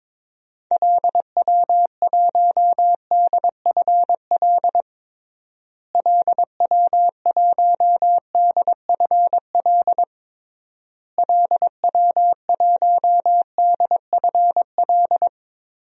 Morse code, heard as a keyed tone, about 21 wpm, 700 Hz. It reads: LW1DFL LW1DFL LW1DFL
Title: DXCC-100 Call Signs - 6 Characters - Repeated 3x - 22wpm